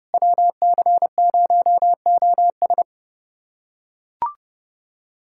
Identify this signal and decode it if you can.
Morse code, heard as a keyed tone, about 30 wpm, 700 Hz. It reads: WC0OS
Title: USA Callsign DE Callsign - 5 Characters - 30wpm